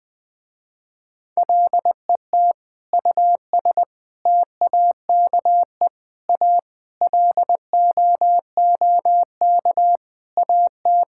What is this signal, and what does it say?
Morse code, heard as a keyed tone, about 21 wpm, 700 Hz. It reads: LET US TAKE A LOOK AT
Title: Sentences with 5-Word Phrases - Spoken Beginning - 20wpm